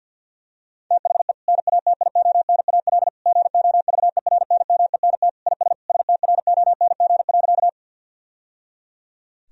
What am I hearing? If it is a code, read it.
Morse code, heard as a keyed tone, about 52 wpm, 700 Hz. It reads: THE NATIONAL GOVERNMENT IS STRONG.
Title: Sentences from Top 1000 Words - Review Part 4 - 50wpm